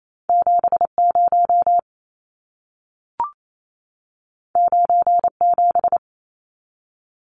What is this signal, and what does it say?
Morse code, heard as a keyed tone, about 28 wpm, 700 Hz. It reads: 70 97
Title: Numbers - 2-Digit 28wpm